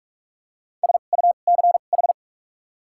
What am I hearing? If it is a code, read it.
Morse code, heard as a keyed tone, about 42 wpm, spaced as for 31 wpm, 700 Hz. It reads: SUCH
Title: 2x Character Spacing - Top 200 Words - 45wpm